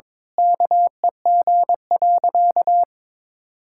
Morse code, heard as a keyed tone, about 22 wpm, 700 Hz. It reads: KEG.
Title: Pangram Challenge v2 - 22wpm